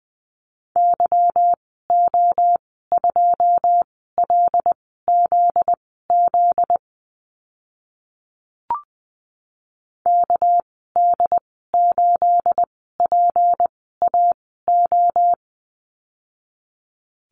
Morse code, heard as a keyed tone, about 20 wpm, spaced as for 14 wpm, 700 Hz. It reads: YO2LZZ KD8PAO
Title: 245 Lesson 39g - Letter z - Call Signs 20wpm